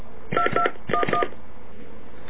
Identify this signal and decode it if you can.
DTMF, heard as telephone touch tones, tones of about 100 ms. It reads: 2211